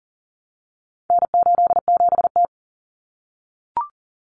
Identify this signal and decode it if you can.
Morse code, heard as a keyed tone, about 40 wpm, 700 Hz. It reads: N87T
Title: Common Cut Numbers - 4-Digit 40wpm